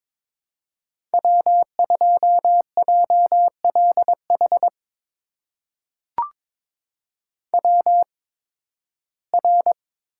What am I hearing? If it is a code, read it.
Morse code, heard as a keyed tone, about 22 wpm, spaced as for 14 wpm, 700 Hz. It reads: W2JLH WR
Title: USA Call Signs - 5 Characters - 8x Character Spacing - 22wpm